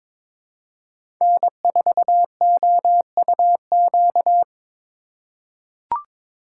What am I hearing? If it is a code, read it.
Morse code, heard as a keyed tone, about 22 wpm, 700 Hz. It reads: N4OUQ
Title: USA Call Signs - 4x Character Spacing - 22wpm